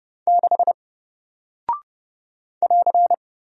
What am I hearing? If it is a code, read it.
Morse code, heard as a keyed tone, about 30 wpm, 700 Hz. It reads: 6 <AR>
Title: Single Character (Letter:Number:Punctuation:Prosign) Alt 30wpm